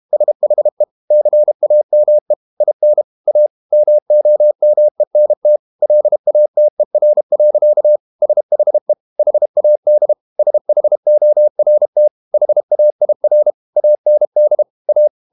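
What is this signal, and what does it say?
Morse code, heard as a keyed tone, about 32 wpm, 600 Hz. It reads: SHE CAME IN A MOMENT LATER. SHE HAD SHORT HAIR AND A
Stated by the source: Morse code practice at 32 wpm from CW med Gnister, reading Ken Follett's novel Never in English.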